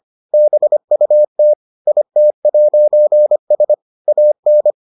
Morse code, heard as a keyed tone, about 25 wpm, 600 Hz. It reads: BUT IT'S AN